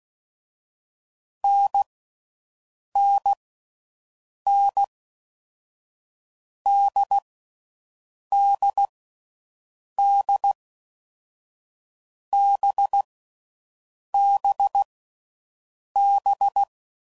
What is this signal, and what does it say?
Morse code, heard as a keyed tone, about 16 wpm, 800 Hz. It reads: N N N D D D B B B
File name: Mixed / Short Runs / NDB NDB6